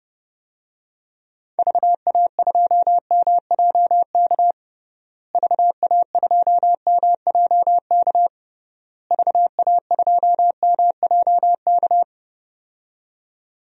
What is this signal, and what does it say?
Morse code, heard as a keyed tone, about 30 wpm, 700 Hz. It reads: VA2MJK VA2MJK VA2MJK
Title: DXCC-100 Call Signs - 6 Characters - Repeated 3x - 30wpm